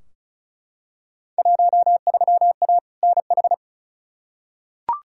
Morse code, heard as a keyed tone, about 34 wpm, 700 Hz. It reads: 13A NH
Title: Field Day Exchanges Encore - 35wpm